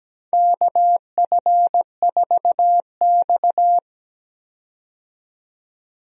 Morse code, heard as a keyed tone, about 17 wpm, 700 Hz. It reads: KF4X
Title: USA Call Signs - 4 Characters - 17wpm